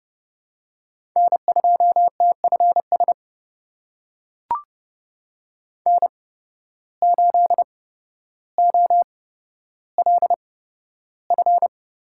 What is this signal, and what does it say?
Morse code, heard as a keyed tone, about 32 wpm, spaced as for 13 wpm, 700 Hz. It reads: N2TFS N8OLF
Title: USA Call Signs - 8x Character Spacing - 30wpm